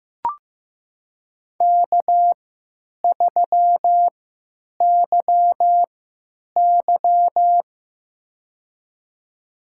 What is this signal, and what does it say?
Morse code, heard as a keyed tone, about 15 wpm, spaced as for 9 wpm, 700 Hz. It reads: K3YY
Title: USA Call Signs - 4 Characters - 3x Character Spacing - 15wpm